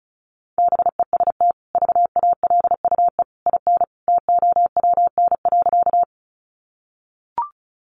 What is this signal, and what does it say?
Morse code, heard as a keyed tone, about 34 wpm, 700 Hz. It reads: BEST VALUE IN TOWN.
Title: Sentences from Top 600 Words 35wpm